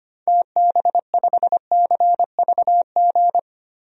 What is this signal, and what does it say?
Morse code, heard as a keyed tone, about 25 wpm, 700 Hz. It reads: TB5CVG
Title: DXCC-100 Call Signs - 6 Characters - 25wpm